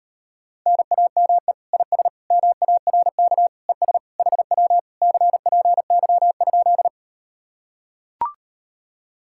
Morse code, heard as a keyed tone, about 39 wpm, 700 Hz. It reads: NAME IS MARK ES HW CPY?